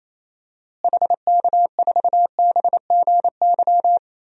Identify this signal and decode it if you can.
Morse code, heard as a keyed tone, about 28 wpm, 700 Hz. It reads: HK4BGY
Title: DXCC-100 Call Signs - 6 Characters - Repeated 3x - 28wpm